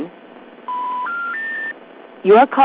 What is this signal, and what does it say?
Special information tone, with speech.
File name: recordings / RE1e0f12afb3e86e c34357d